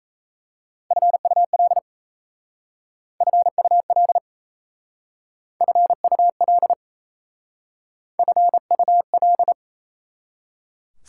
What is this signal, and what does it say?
Morse code, heard as a keyed tone, about 37 wpm, 690 Hz. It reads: FUL FUL FUL FUL